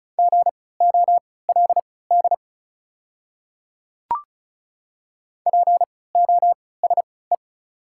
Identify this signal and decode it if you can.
Morse code, heard as a keyed tone, about 36 wpm, spaced as for 20 wpm, 690 Hz. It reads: GOLD POSE
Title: 3x Character Spacing - Top 500 Words - 35wpm